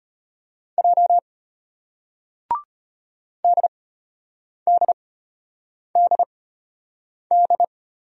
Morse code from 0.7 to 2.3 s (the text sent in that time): J